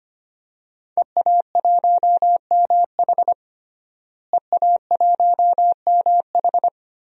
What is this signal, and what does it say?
Morse code, heard as a keyed tone, about 25 wpm, 700 Hz. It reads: EA1MH EA1MH
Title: DXCC-100 Call Signs - 5 Characters - Repeated 3x - 25wpm